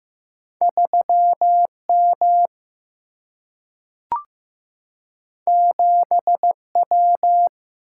Morse code, heard as a keyed tone, about 15 wpm, 700 Hz. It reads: 3M 7W
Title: Single-Letter Single-Number 15wpm